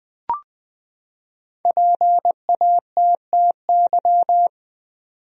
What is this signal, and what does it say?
Morse code, heard as a keyed tone, about 20 wpm, 700 Hz. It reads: PATTY
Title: USA Common Names - Max 5 Letters - 20wpm